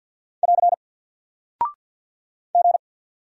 Morse code, heard as a keyed tone, about 48 wpm, 700 Hz. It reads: <AR> G
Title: Single Character (Letter:Number:Punctuation:Prosign) Alt 50wpm